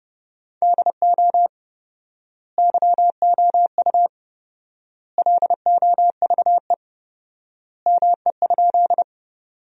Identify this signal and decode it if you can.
Morse code, heard as a keyed tone, about 30 wpm, 700 Hz. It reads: DO YOU LOVE ME?
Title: 4x Word-Spacing - Sentences from Top 400 Words - 30wpm